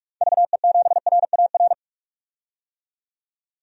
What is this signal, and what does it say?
Morse code, heard as a keyed tone, about 46 wpm, 700 Hz. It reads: VE7RAR
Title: DXCC-100 Call Signs - 6 Characters - 45wpm